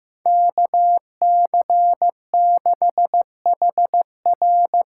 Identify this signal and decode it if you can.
Morse code, heard as a keyed tone, about 15 wpm, 700 Hz. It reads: KC6HR
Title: USA Call Signs - 6 Characters - 6x Character Spacing - 15wpm